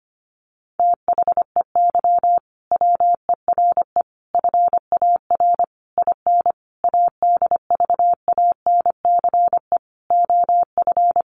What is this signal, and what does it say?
Morse code, heard as a keyed tone, about 25 wpm, 700 Hz. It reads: THEY WERE FAR IN ADVANCE OF
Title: Sentences from Top 1400 Words 25wpm